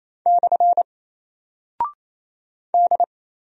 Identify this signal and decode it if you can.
Morse code, heard as a keyed tone, about 28 wpm, 700 Hz. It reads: / D
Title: Single Character (Letter:Number:Punctuation:Prosign) Alt 28wpm